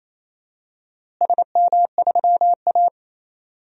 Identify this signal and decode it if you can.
Morse code, heard as a keyed tone, about 28 wpm, 700 Hz. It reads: SM3A
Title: DXCC-100 Call Signs - 4 Characters - Repeated 3x - 28wpm